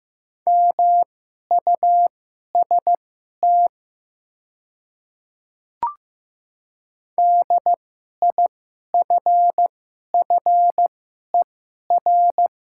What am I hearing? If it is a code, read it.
Morse code, heard as a keyed tone, about 15 wpm, spaced as for 11 wpm, 700 Hz. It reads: MUST DIFFER
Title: 2x Character Spacing - Top 200 Words - 15wpm